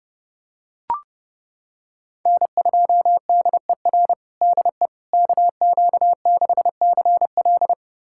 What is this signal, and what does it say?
Morse code, heard as a keyed tone, about 30 wpm, 700 Hz. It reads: N2DER DE KQ6CL
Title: USA Callsign DE Callsign - 5 Characters - 30wpm